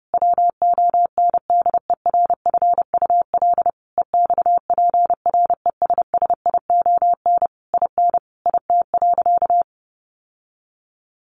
Morse code, heard as a keyed tone, about 28 wpm, 700 Hz. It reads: WONDERFUL EXPRESSION IN IT.
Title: Sentences from Top 1200 Words 30wpm